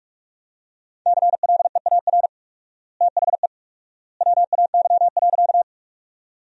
Morse code, heard as a keyed tone, about 41 wpm, 700 Hz. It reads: CLEAR THE WAY.
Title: 4x Word-Spacing - Sentences from Top 500 Words - 45wpm